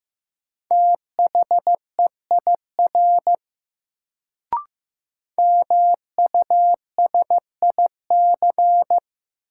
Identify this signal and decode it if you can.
Morse code, heard as a keyed tone, about 15 wpm, 700 Hz. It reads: THEIR MUSIC
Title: Sentences from Top 300 Words - New Word Drill - 15wpm